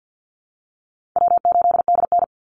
Morse code, heard as a keyed tone, about 50 wpm, 700 Hz. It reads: W8DN
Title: Call Signs Repeated 2x - 50wpm